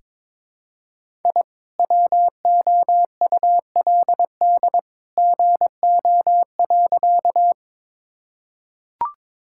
Morse code, heard as a keyed tone, about 22 wpm, 700 Hz. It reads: I WOULD GO.
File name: practice / 22wpm / Sentences with 2-Word Phrases - 22wpm